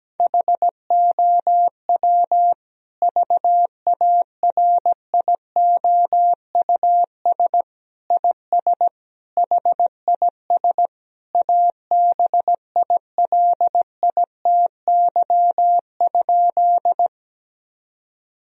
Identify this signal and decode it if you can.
Morse code, heard as a keyed tone, about 17 wpm, 700 Hz. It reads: HOW VARIOUS IS HIS ABILITY?
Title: Sentences from Top 1000 Words - Review Part 4 - 17wpm